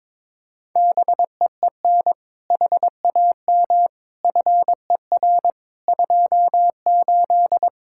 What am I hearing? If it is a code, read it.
Morse code, heard as a keyed tone, about 22 wpm, 700 Hz. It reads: BEEN HAM FER 28
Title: Practice QSO - Exchange 3 - More Info - 22wpm